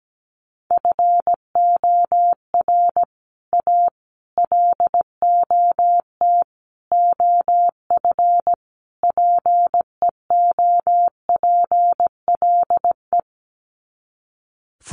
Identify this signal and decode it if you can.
Morse code, heard as a keyed tone, about 17 wpm, 700 Hz. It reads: FOR A LOT OF PEOPLE
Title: Sets of 5 Words Encore 17wpm